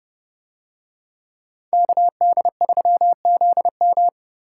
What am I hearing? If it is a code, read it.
Morse code, heard as a keyed tone, about 29 wpm, 700 Hz. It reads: KD3ZM